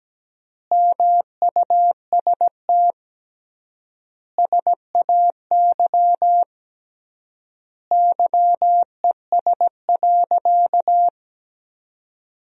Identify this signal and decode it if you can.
Morse code, heard as a keyed tone, about 17 wpm, 700 Hz. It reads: MUST SAY YES.
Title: 3x Word-Spacing - Sentences from Top 500 Words - 17wpm